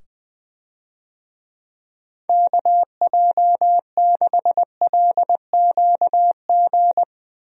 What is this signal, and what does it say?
Morse code, heard as a keyed tone, about 20 wpm, 700 Hz. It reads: KJ6LQG
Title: USA Call Signs - 6 Characters - 20wpm